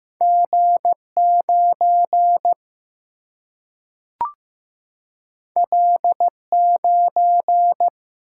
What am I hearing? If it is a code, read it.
Morse code, heard as a keyed tone, about 15 wpm, 700 Hz. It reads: G9 L9
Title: Single-Letter Single-Number 15wpm